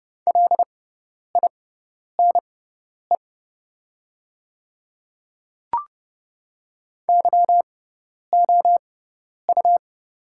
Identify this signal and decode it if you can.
Morse code, heard as a keyed tone, about 30 wpm, spaced as for 10 wpm, 700 Hz. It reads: LINE YOU
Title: 6x Character Spacing - Top 200 Words - 30wpm